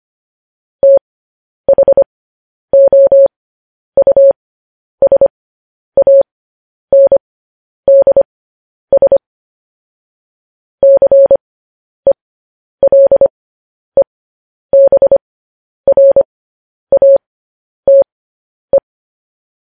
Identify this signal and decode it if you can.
Morse code, heard as a keyed tone, about 25 wpm, spaced as for 10 wpm, 550 Hz. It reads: THOUSANDS CELEBRATE
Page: news-headlines-in-morse-code-at-10-wpm